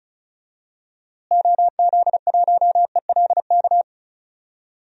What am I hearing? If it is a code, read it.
Morse code, heard as a keyed tone, about 36 wpm, 700 Hz. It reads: OZ1ELK
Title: DXCC-100 Call Signs - 6 Characters - Repeated 2x - 35wpm